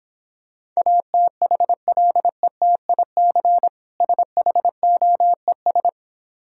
Morse code, heard as a keyed tone, about 26 wpm, 700 Hz. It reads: ATHLETIC SHOES